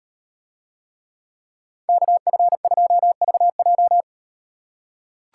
Morse code, heard as a keyed tone, about 38 wpm, 700 Hz. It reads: KF2VJ